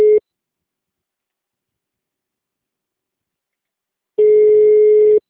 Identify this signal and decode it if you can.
Ringback tone (Europe, 425 Hz).